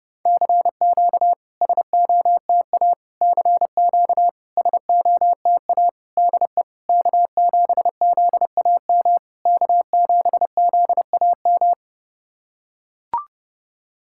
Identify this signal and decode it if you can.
Morse code, heard as a keyed tone, about 30 wpm, 690 Hz. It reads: CQ SOTA CQ SOTA DE K7ZAM K7ZAM
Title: SOTA QSO Practice 30wpm